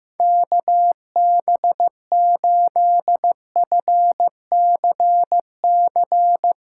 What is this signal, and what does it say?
Morse code, heard as a keyed tone, about 16 wpm, 700 Hz. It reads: KB8FCC
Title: USA Callsign DE Callsign 15wpm